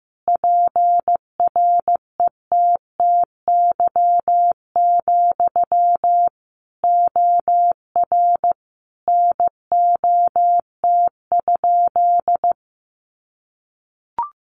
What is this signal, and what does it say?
Morse code, heard as a keyed tone, about 15 wpm, 700 Hz. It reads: PRETTY, OR NOT?
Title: Sentences from Top 1000 Words 15wpm